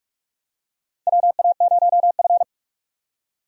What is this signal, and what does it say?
Morse code, heard as a keyed tone, about 46 wpm, 700 Hz. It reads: WA0F
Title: USA Call Signs - 4 Characters - 45wpm